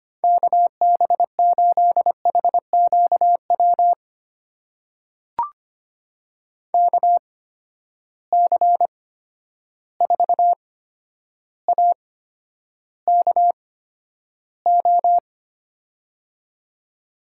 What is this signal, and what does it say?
Morse code, heard as a keyed tone, about 25 wpm, spaced as for 11 wpm, 700 Hz. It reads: KB8HQW KC4AKO
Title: USA Call Signs - 6 Characters - 8x Character Spacing - 25wpm